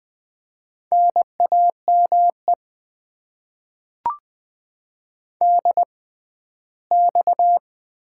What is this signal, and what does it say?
Morse code, heard as a keyed tone, about 20 wpm, spaced as for 14 wpm, 700 Hz. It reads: NAME DX
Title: Top 90 QSO Elements - 6x Character Spacing - 20wpm